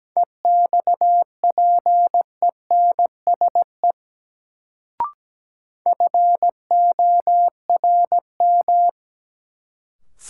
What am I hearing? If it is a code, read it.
Morse code, heard as a keyed tone, about 17 wpm, 700 Hz. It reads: EXPENSE FORM
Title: Sets of 4 Words - New Word Drill - 17wpm